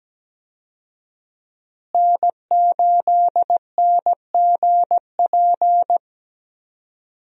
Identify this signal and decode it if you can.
Morse code, heard as a keyed tone, about 17 wpm, 700 Hz. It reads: N8NGP
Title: USA Call Signs - 5 Characters - Repeated 2x - 17wpm